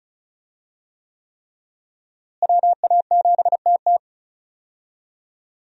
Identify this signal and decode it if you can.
Morse code, heard as a keyed tone, about 34 wpm, 700 Hz. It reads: WA7TT
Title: USA Call Signs - 5 Characters - Repeated 3x - 35wpm